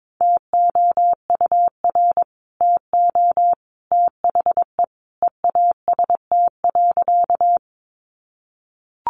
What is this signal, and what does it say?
Morse code, heard as a keyed tone, about 22 wpm, 700 Hz. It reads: TOUR TO THE EAST.
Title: Sentences from Top 1400 Words 22wpm